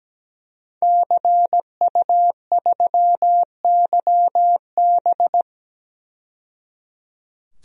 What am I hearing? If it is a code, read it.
Morse code, heard as a keyed tone, about 17 wpm, 700 Hz. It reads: CU3YB
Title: DXCC-100 Call Signs - 5 Characters - 17wpm